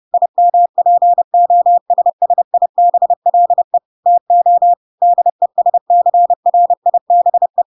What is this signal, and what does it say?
Morse code, heard as a keyed tone, about 30 wpm, 700 Hz. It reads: IMPOSSIBLE TO DESCRIBE
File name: Pinocchio / Pinocchio ch5 30 30 ik1hge